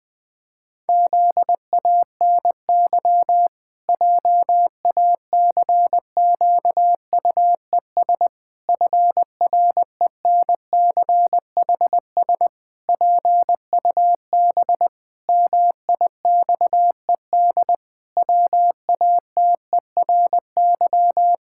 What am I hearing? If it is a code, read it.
Morse code, heard as a keyed tone, about 20 wpm, 700 Hz. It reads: ZANY JACQUES FRENCHS PUB MIXED WATERY
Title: Pangram Challenge v2 - 20wpm